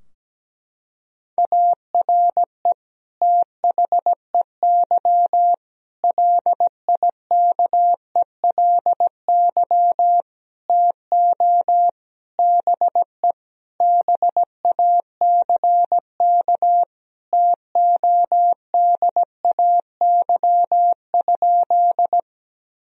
Morse code, heard as a keyed tone, about 17 wpm, 700 Hz. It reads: ARE THEY LIKELY TO BE BACK TODAY?
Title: Sentences from Top 1000 Words - Review Part 4 - 17wpm